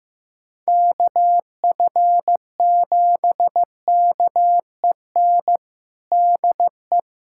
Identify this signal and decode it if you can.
Morse code, heard as a keyed tone, about 15 wpm, 700 Hz. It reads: KF7KEN DE
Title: USA Callsign DE Callsign - 6 Characters - 15wpm